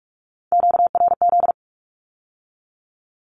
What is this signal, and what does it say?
Morse code, heard as a keyed tone, about 46 wpm, 700 Hz. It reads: QRZ
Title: Q-Codes with Spoken Meaning - 45wpm